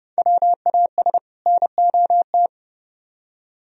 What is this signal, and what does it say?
Morse code, heard as a keyed tone, about 36 wpm, 700 Hz. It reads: WAS NOT
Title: Speed-Racing - Sets of 2 Words 30wpm